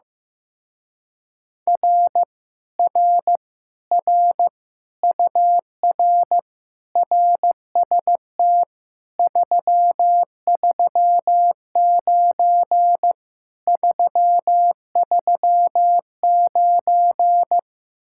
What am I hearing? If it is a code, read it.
Morse code, heard as a keyed tone, about 15 wpm, 700 Hz. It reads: R R R UR RST 339 339
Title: Speed-Racing Common QSO Phrases Encore - 15wpm